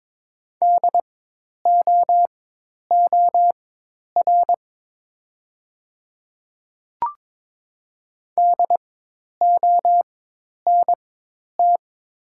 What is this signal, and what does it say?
Morse code, heard as a keyed tone, about 22 wpm, spaced as for 10 wpm, 700 Hz. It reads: DOOR DONT
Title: 4x Character Spacing - Top 300 Words - 22wpm